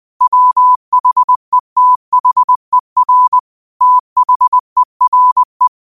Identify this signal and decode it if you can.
Morse code, heard as a keyed tone, about 20 wpm, 1000 Hz. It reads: WHETHER THERE
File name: qotdfiles / 20wpm